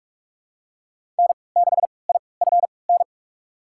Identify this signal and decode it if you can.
Morse code, heard as a keyed tone, about 47 wpm, spaced as for 26 wpm, 700 Hz. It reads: N6IFN
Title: USA Call Signs - 5 Characters - 3x Character Spacing - 45wpm